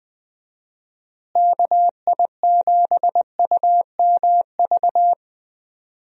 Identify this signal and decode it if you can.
Morse code, heard as a keyed tone, about 20 wpm, 700 Hz. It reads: KI7UMV
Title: USA Call Signs - 6 Characters - Repeated 3x - 20wpm